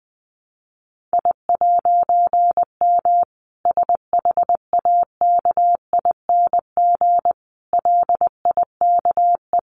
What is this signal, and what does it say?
Morse code, heard as a keyed tone, about 20 wpm, 700 Hz. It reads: I'M SHAKING LIKE
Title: A Sound Of Thunder 20wpm